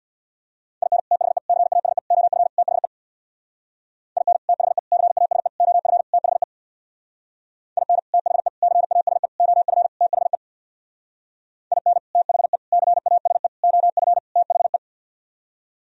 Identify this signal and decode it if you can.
Morse code, heard as a keyed tone, about 64 wpm, 700 Hz. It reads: IN THE CASE OF THE IN THE CASE OF THE IN THE CASE OF THE IN THE CASE OF THE